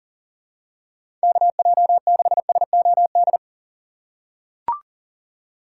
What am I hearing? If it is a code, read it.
Morse code, heard as a keyed tone, about 40 wpm, 700 Hz. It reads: KJ6SOD
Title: USA Call Signs - 6x Character Spacing - 40wpm